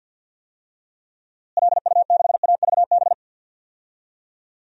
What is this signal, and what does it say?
Morse code, heard as a keyed tone, about 46 wpm, 700 Hz. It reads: LU6AVB
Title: DXCC-100 Call Signs - 6 Characters - Repeated 3x - 50wpm